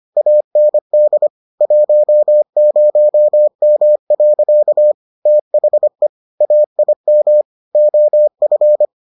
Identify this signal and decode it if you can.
Morse code, heard as a keyed tone, about 25 wpm, 600 Hz. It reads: AND 10M. THE AIM OF